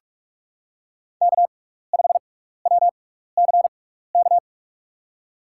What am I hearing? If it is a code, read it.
Morse code, heard as a keyed tone, about 44 wpm, spaced as for 15 wpm, 700 Hz. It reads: K5WCK
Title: USA Call Signs - 6x Character Spacing - 45wpm